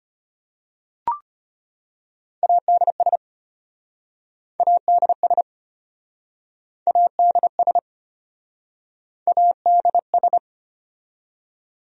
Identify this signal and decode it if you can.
Morse code, heard as a keyed tone, about 34 wpm, 700 Hz. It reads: ADS ADS ADS ADS